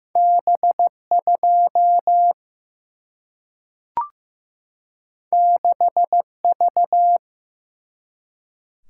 Morse code, heard as a keyed tone, about 15 wpm, 700 Hz. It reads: B2 6V
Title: Single-Letter Single-Number 15wpm